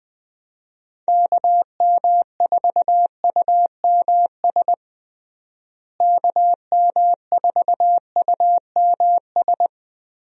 Morse code, heard as a keyed tone, about 21 wpm, 700 Hz. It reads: KM4UMS KM4UMS
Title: USA Call Signs - 6 Characters - Repeated 2x - 20wpm